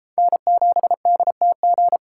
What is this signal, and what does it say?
Morse code, heard as a keyed tone, about 32 wpm, 700 Hz. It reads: N7DTG